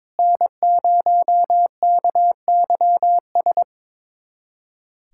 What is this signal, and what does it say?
Morse code, heard as a keyed tone, about 22 wpm, 700 Hz. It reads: N0KYS
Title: USA Callsign DE Callsign - 5 Characters - 22wpm